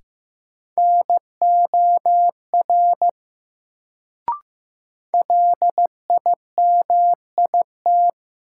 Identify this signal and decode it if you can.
Morse code, heard as a keyed tone, about 15 wpm, 700 Hz. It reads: NOR LIMIT
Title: Sentences from Top 1500 Words - New Word Drill - 15wpm